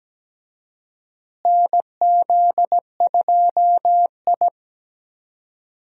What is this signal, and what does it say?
Morse code, heard as a keyed tone, about 17 wpm, 700 Hz. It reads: NZ2I
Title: USA Call Signs - 4 Characters - Repeated 2x - 17wpm